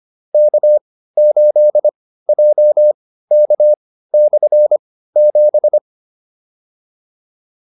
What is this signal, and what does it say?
Morse code, heard as a keyed tone, about 25 wpm, spaced as for 15 wpm, 600 Hz. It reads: K8JK/7